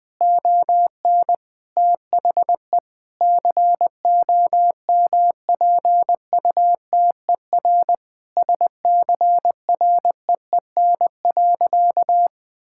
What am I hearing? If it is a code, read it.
Morse code, heard as a keyed tone, about 20 wpm, 700 Hz. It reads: ON THE COMPUTER SCREEN.
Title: Sentences from Top 1000 Words - Review Part 6 - 20wpm